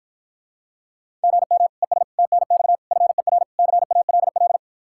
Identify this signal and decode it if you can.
Morse code, heard as a keyed tone, about 53 wpm, 700 Hz. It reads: GM ES TNX FER CALL